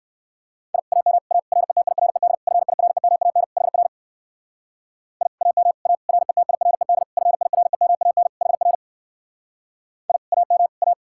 Morse code, heard as a keyed tone, about 55 wpm, 700 Hz. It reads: I AM A RETIRED FIREMAN HR I AM A RETIRED FIREMAN HR I AM A
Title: Speed-Racing Common QSO Phrases - 45wpm